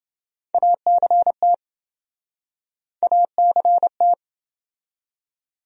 Morse code, heard as a keyed tone, about 27 wpm, 700 Hz. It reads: ACT ACT